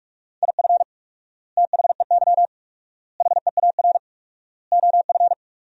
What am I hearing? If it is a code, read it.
Morse code, heard as a keyed tone, about 41 wpm, 700 Hz. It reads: IF THEY HEAR OF
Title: 4x Word-Spacing - Sentences from Top 400 Words - 45wpm